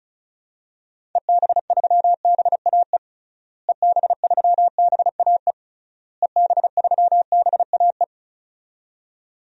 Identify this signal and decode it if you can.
Morse code, heard as a keyed tone, about 34 wpm, 700 Hz. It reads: EB3BAE EB3BAE EB3BAE
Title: DXCC-100 Call Signs - 6 Characters - Repeated 3x - 35wpm